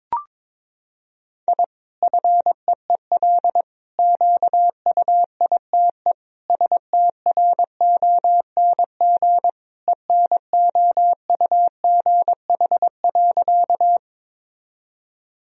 Morse code, heard as a keyed tone, about 22 wpm, 700 Hz. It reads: I FEEL QUITE STRONG ENOUGH.
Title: Sentences from Top 1000 Words - Review Part 3 - 22wpm